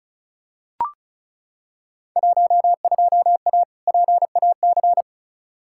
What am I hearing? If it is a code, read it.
Morse code, heard as a keyed tone, about 34 wpm, 700 Hz. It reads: 12A PAC
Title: Field Day Exchanges Encore - 35wpm